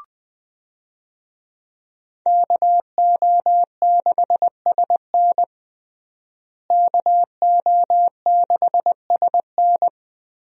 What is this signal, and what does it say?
Morse code, heard as a keyed tone, about 21 wpm, 700 Hz. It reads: KO6SN KO6SN
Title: USA Call Signs - 5 Characters - Repeated 3x - 20wpm